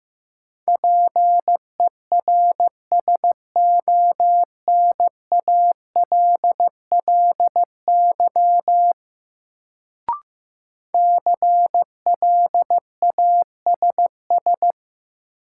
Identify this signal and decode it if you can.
Morse code, heard as a keyed tone, about 15 wpm, 700 Hz. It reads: PERSONALLY CLASS
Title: Sentences from Top 1400 Words - New Word Drill Alt - 15wpm